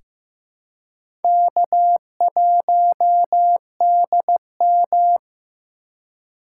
Morse code, heard as a keyed tone, about 15 wpm, 700 Hz. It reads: K1DM
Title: K1USN SST QSO Practice - 15wpm